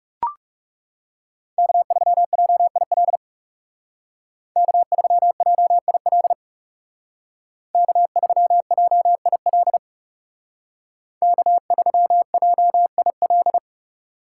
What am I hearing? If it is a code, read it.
Morse code, heard as a keyed tone, about 39 wpm, 700 Hz. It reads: K3JIL K3JIL K3JIL K3JIL